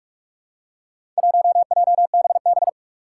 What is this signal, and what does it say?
Morse code, heard as a keyed tone, about 44 wpm, 700 Hz. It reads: 1JBB
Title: Four Challenging Letter-Number Combinations 45wpm